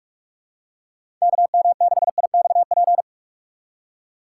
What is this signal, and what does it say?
Morse code, heard as a keyed tone, about 44 wpm, 700 Hz. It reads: KM6IXP
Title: USA Call Signs - 6x Character Spacing - 45wpm